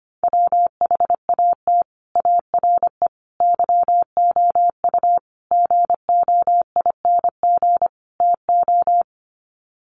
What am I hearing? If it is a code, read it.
Morse code, heard as a keyed tone, about 25 wpm, 700 Hz. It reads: WHAT ARE YOU GOING TO
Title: Sets of 5 Words 25wpm